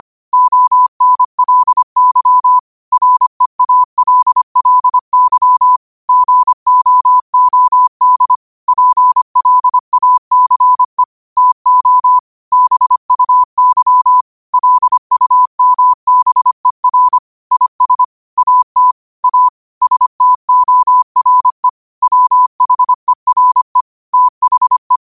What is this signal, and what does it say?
Morse code, heard as a keyed tone, about 25 wpm, 1000 Hz. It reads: ONLY REALLY GOOD PLACE TO BUY LUMBER IS AT A STORE WHERE THE
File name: qotdfiles / 25wpm